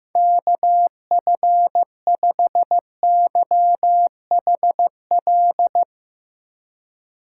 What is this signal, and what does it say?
Morse code, heard as a keyed tone, about 15 wpm, 700 Hz. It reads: KF5YHL
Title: USA Call Signs - 6 Characters - 3x Character Spacing - 15wpm